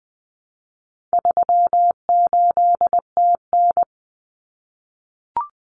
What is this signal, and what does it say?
Morse code, heard as a keyed tone, about 20 wpm, 700 Hz. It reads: 38TN
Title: Common Cut Numbers - 4-Digit 20wpm